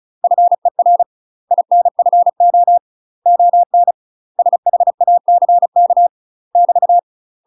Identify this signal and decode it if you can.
Morse code, heard as a keyed tone, about 34 wpm, 700 Hz. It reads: FER INFO ON SHACK <BT>
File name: UK MASTER QSOs TRAINING / 35WPM / UK MASTER QSOs 0037